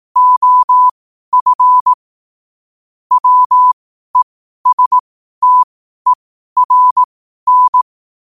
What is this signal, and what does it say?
Morse code, heard as a keyed tone, about 18 wpm, spaced as for 12 wpm, 1000 Hz. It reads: OF WESTERN
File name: qotdfiles / 12wpm